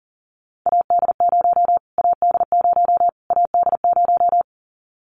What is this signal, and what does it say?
Morse code, heard as a keyed tone, about 40 wpm, 700 Hz. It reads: AD0 AD0 AD0
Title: Callsign Prefixes for Top 25 DXCC - Repeated 3x - 40wpm